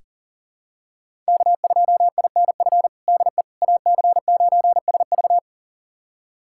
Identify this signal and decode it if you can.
Morse code, heard as a keyed tone, about 40 wpm, 700 Hz. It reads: K2INF DE AC9SV
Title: USA Callsign DE Callsign - 5 Characters - 40wpm